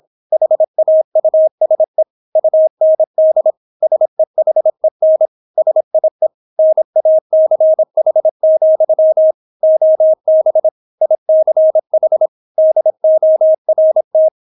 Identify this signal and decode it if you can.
Morse code, heard as a keyed tone, about 26 wpm, 630 Hz. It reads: HAUSE UND SEHEN SIE NACH, OB ICH DORT